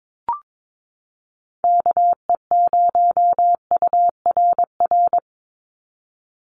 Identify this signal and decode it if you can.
Morse code, heard as a keyed tone, about 22 wpm, 700 Hz. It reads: KE0URR
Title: USA Call Signs 22wpm